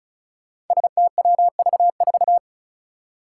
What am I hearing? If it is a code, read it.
Morse code, heard as a keyed tone, about 36 wpm, 700 Hz. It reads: STWV4
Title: Five Challenging Letter-Number Combinations 35wpm